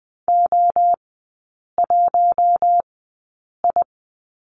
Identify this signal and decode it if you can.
Morse code, heard as a keyed tone, about 20 wpm, spaced as for 8 wpm, 700 Hz. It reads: O1I
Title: 053 Lesson 10e - Letter r - Three Characters 20wpm